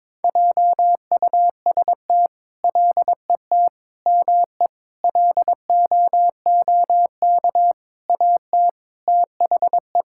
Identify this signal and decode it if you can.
Morse code, heard as a keyed tone, about 21 wpm, 700 Hz. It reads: JUST LET ME LOOK AT THE
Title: Sentences from Top 1000 Words - Review Part 5 - 22wpm